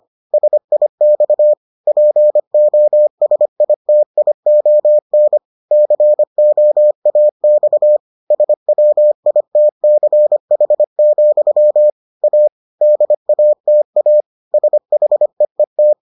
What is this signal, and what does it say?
Morse code, heard as a keyed tone, about 25 wpm, 600 Hz. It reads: SIX POSITION COAX SWITCH, A DATA SHEET